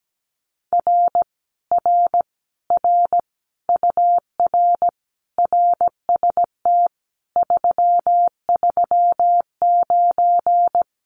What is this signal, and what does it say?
Morse code, heard as a keyed tone, about 17 wpm, 700 Hz. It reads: R R R UR RST 339
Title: Common QSO Phrases Encore 17wpm